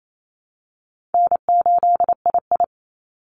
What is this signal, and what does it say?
Morse code, heard as a keyed tone, about 28 wpm, 700 Hz. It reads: N8II
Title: Easy Call Signs - 28wpm